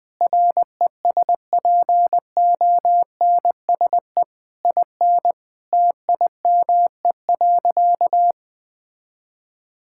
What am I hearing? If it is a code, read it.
Morse code, heard as a keyed tone, about 20 wpm, 700 Hz. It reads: RESPONSE IN TIME.
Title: Sentences with 8-Word Phrases - 20wpm